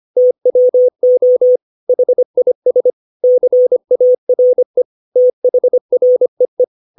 Morse code, heard as a keyed tone, about 25 wpm, 500 Hz. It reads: TWO HIS CARE THREE
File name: 632 words / 632 Words 25WPM